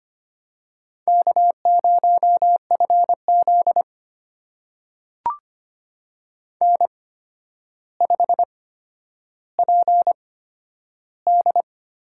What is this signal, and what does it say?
Morse code, heard as a keyed tone, about 25 wpm, spaced as for 11 wpm, 700 Hz. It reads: K0FZ N5PD
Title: USA Call Signs - 8x Character Spacing - 25wpm